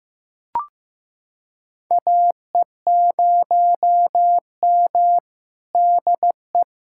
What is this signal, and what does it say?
Morse code, heard as a keyed tone, about 15 wpm, 700 Hz. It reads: AE0M DE
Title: USA Callsign DE Callsign - 4 Characters - 15wpm